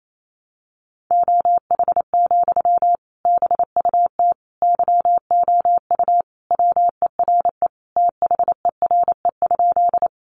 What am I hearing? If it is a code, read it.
Morse code, heard as a keyed tone, about 28 wpm, 700 Hz. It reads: OH, BUT YOU WERE THERE?
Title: Sentences from Top 500 Words 28wpm